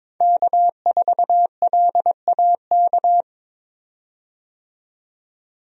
Morse code, heard as a keyed tone, about 22 wpm, 700 Hz. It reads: K4LAK
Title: USA Call Signs - 5 Characters - 22wpm